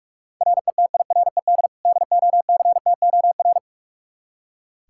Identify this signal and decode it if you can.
Morse code, heard as a keyed tone, about 45 wpm, 700 Hz. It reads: RETIRED DOCTOR
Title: Practice QSO - Exchange 3 - More Info - 45wpm